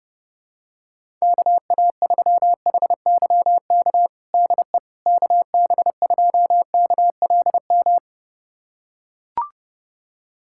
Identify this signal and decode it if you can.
Morse code, heard as a keyed tone, about 30 wpm, 700 Hz. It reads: KA3HYK DE KB2KLM
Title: USA Callsign DE Callsign - 6 Characters - 30wpm